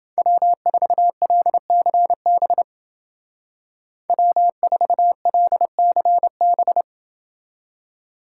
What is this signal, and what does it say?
Morse code, heard as a keyed tone, about 29 wpm, 700 Hz. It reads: W4LCB W4LCB